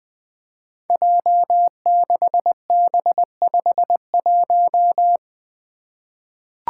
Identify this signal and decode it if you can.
Morse code, heard as a keyed tone, about 20 wpm, 700 Hz. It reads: J6B51
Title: Five Challenging Letter-Number Combinations 20wpm